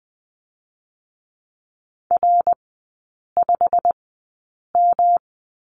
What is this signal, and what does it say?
Morse code, heard as a keyed tone, about 20 wpm, spaced as for 8 wpm, 690 Hz. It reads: R5M
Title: 172 Lesson 28e - Number 9 - Three Characters 20wpm